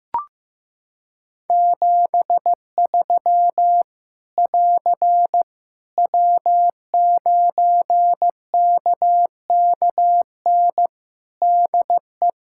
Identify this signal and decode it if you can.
Morse code, heard as a keyed tone, about 15 wpm, 700 Hz. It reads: 73 <AR> W9KKN DE
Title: K1USN SST QSO Practice 15wpm